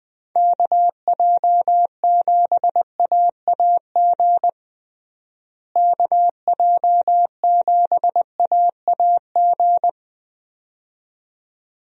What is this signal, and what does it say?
Morse code, heard as a keyed tone, about 20 wpm, 700 Hz. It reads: KJ7AAG KJ7AAG
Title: USA Call Signs - 6 Characters - Repeated 2x - 20wpm